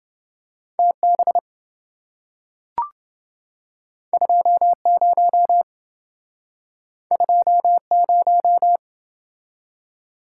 Morse code, heard as a keyed tone, about 27 wpm, 700 Hz. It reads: TB 20 20